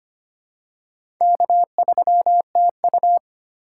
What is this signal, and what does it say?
Morse code, heard as a keyed tone, about 25 wpm, 700 Hz. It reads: K3TU
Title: USA Call Signs - 2x Character Spacing - 25wpm